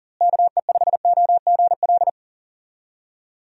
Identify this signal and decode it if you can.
Morse code, heard as a keyed tone, about 40 wpm, 700 Hz. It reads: KE5OGL
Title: USA Call Signs - 6 Characters - Repeated 2x - 40wpm